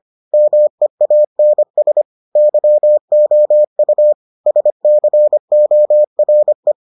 Morse code, heard as a keyed tone, about 25 wpm, 600 Hz. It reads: MEANS YOU SCORE